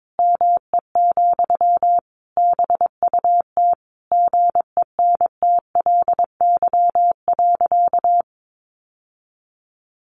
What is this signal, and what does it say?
Morse code, heard as a keyed tone, about 22 wpm, 700 Hz. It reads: ME, BUT GENTLY.
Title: Sentences from Top 1500 Words 22wpm